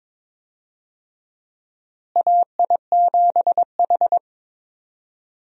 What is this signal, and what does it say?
Morse code, heard as a keyed tone, about 22 wpm, 700 Hz. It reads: AI7H
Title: USA Call Signs - 4 Characters - Repeated 2x - 22wpm